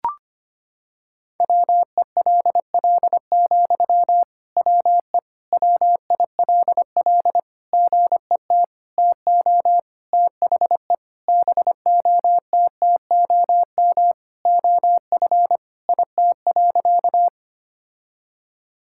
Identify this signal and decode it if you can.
Morse code, heard as a keyed tone, about 25 wpm, 700 Hz. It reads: WELL, WE WILL GET TO THE BOTTOM OF IT.
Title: Sentences from Top 1000 Words - Review Part 6 - 25wpm